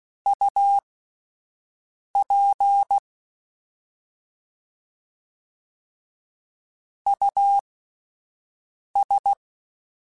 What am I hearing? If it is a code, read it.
Morse code, heard as a keyed tone, about 16 wpm, spaced as for 5 wpm, 800 Hz. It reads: UP US
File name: Alphabetical / Alphabetical mp3s / Us @5